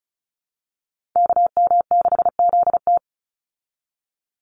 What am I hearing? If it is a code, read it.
Morse code, heard as a keyed tone, about 34 wpm, 700 Hz. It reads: KM6ZT
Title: Call Signs Repeated 2x - 35wpm